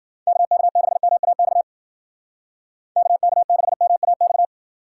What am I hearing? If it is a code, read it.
Morse code, heard as a keyed tone, about 53 wpm, 700 Hz. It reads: KK6MAX KK6MAX